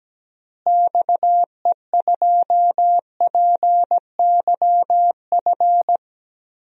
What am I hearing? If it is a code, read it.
Morse code, heard as a keyed tone, about 17 wpm, 700 Hz. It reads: XE2PYF
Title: DXCC-100 Call Signs - 6 Characters - Repeated 2x - 17wpm